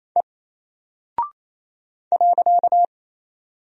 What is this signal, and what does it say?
Morse code, heard as a keyed tone, about 28 wpm, 700 Hz. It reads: E .
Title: Single Character (Letter:Number:Punctuation:Prosign) 28wpm